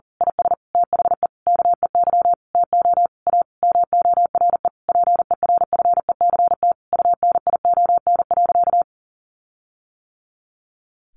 Morse code, heard as a keyed tone, about 40 wpm, 700 Hz. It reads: IS THE KEY TO A MORE PERFECT UNION.
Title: Sentences from Top 1400 Words 40wpm